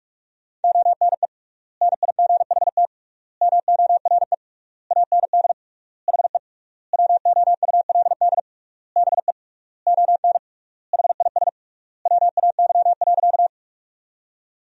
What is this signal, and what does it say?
Morse code, heard as a keyed tone, about 42 wpm, 700 Hz. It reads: ONE NIGHT MORE AND HE WOULD BE ON HIS WAY.
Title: 3x Word-Spacing - Sentences from Top 300 Words - 45wpm